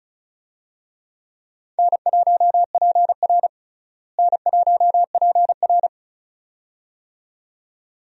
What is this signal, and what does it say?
Morse code, heard as a keyed tone, about 36 wpm, 700 Hz. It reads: N1PR N1PR
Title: USA Call Signs - 4 Characters - Repeated 2x - 35wpm